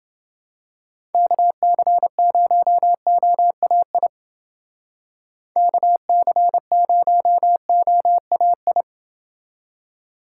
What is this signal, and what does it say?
Morse code, heard as a keyed tone, about 28 wpm, 700 Hz. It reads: KC0OAI KC0OAI